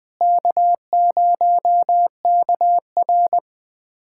Morse code, heard as a keyed tone, about 20 wpm, 700 Hz. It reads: K0KR
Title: USA Callsign DE Callsign - 4 Characters - 20wpm